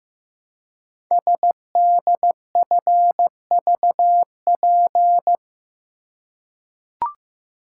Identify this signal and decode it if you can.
Morse code, heard as a keyed tone, about 15 wpm, 700 Hz. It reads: SDFVP
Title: Code Groups - 5 Letter:Numbers - 15wpm